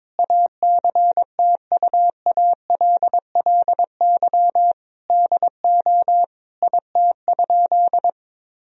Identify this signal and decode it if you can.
Morse code, heard as a keyed tone, about 22 wpm, 700 Hz. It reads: ACTUALLY DO IT?
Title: Sentences from Top 1000 Words - Review Part 4 - 22wpm